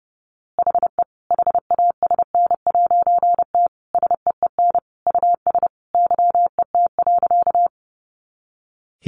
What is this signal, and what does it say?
Morse code, heard as a keyed tone, about 30 wpm, 700 Hz. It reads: HE HASN'T SEEN US YET.
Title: A Sound Of Thunder 30wpm